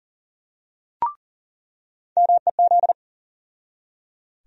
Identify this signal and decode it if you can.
Morse code, heard as a keyed tone, about 40 wpm, 700 Hz. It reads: MEZ
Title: USA Call Sign Postfixes - 3 Letters 40wpm